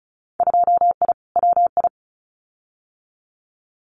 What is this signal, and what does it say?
Morse code, heard as a keyed tone, about 36 wpm, 690 Hz. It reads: 2I WI
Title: Winter Field Day Exchanges 35wpm